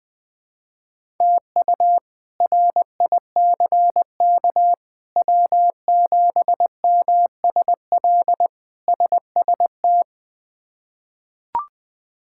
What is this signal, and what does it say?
Morse code, heard as a keyed tone, about 20 wpm, 700 Hz. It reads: TU RICK W7MSL SST
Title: K1USN SST QSO Practice - 20wpm